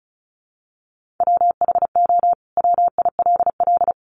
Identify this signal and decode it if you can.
Morse code, heard as a keyed tone, about 34 wpm, 700 Hz. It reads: WHO WILL
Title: Sentences from Top 800 Words 35wpm